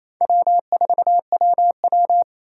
Morse code, heard as a keyed tone, about 28 wpm, 700 Hz. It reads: W4WW
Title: USA Call Signs - 4 Characters - 8x Character Spacing - 28wpm